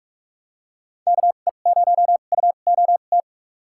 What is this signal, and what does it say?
Morse code, heard as a keyed tone, about 46 wpm, spaced as for 32 wpm, 700 Hz. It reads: KE0UOT
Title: USA Call Signs - 6 Characters - 2x Character Spacing - 45wpm